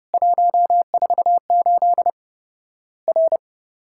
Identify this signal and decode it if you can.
Morse code, heard as a keyed tone, about 32 wpm, 700 Hz. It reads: 148 R
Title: Worked All Europe DX Contest - QTC Exchange - 30wpm